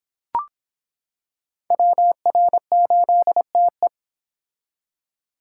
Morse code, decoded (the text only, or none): WR8TE